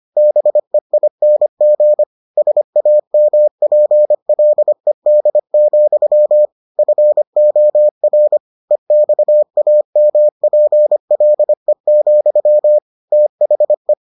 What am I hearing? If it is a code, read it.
Morse code, heard as a keyed tone, about 25 wpm, 600 Hz. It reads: BEING SAMPLED, FOR EXAMPLE, THE